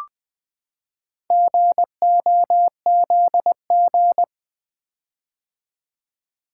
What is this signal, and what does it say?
Morse code, heard as a keyed tone, about 20 wpm, 700 Hz. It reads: GOZG
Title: Four Letter-Number Combinations - 20wpm